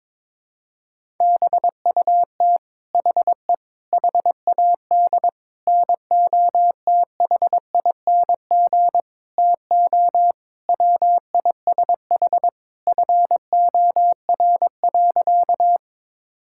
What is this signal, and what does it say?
Morse code, heard as a keyed tone, about 22 wpm, 700 Hz. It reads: BUT HE HAD NOTHING TO WISH FOR.
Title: Sentences from Top 1600 Words 22wpm